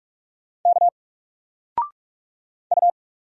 Morse code, heard as a keyed tone, about 44 wpm, 700 Hz. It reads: K U
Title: Single Letter-Number 45wpm